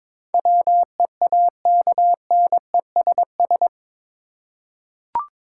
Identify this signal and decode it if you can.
Morse code, heard as a keyed tone, about 22 wpm, 700 Hz. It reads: WEAKNESS
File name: practice / 22wpm / Eight Letter Words - 22wpm